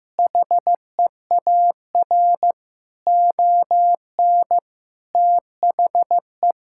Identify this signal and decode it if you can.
Morse code, heard as a keyed tone, about 15 wpm, 700 Hz. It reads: HEAR ON THE
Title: Sentences from Top 1000 Words - Review Part 6 - 15wpm